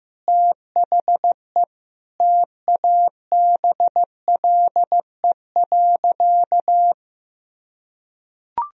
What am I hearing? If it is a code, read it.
Morse code, heard as a keyed tone, about 15 wpm, 700 Hz. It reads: THE TABLE.
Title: Sentences from Top 1000 Words - Review Part 4 - 15wpm